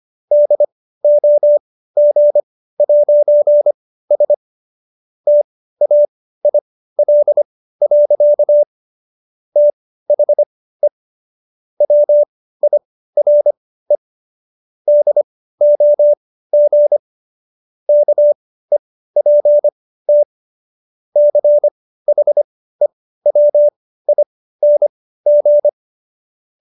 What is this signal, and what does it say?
Morse code, heard as a keyed tone, about 25 wpm, spaced as for 15 wpm, 590 Hz. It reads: DOG'S TAIL. THE WIRE DOG KEPT CHEWING